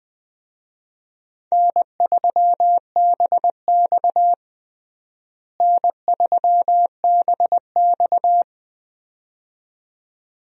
Morse code, heard as a keyed tone, about 20 wpm, 700 Hz. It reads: N3BX N3BX
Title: USA Call Signs - 4 Characters - Repeated 2x - 20wpm